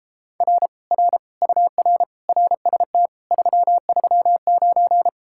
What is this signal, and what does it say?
Morse code, heard as a keyed tone, about 33 wpm, 700 Hz. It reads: R R UR RST 339